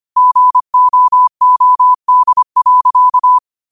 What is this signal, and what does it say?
Morse code, heard as a keyed tone, about 25 wpm, 1000 Hz. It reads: GOOD.